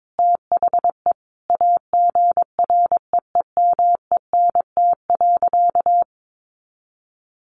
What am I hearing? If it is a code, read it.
Morse code, heard as a keyed tone, about 22 wpm, 700 Hz. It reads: THE AGREEMENT.
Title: Sentences from Top 1500 Words 22wpm